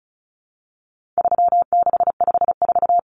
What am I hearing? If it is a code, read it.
Morse code, heard as a keyed tone, about 34 wpm, 700 Hz. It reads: 3654
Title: Common Cut Numbers - 4-Digit 35wpm